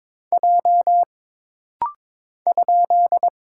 Morse code, heard as a keyed tone, about 21 wpm, 700 Hz. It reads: J ?
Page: This rapid-fire timing is geared to push your ICR (Instant Character Recognition) skills.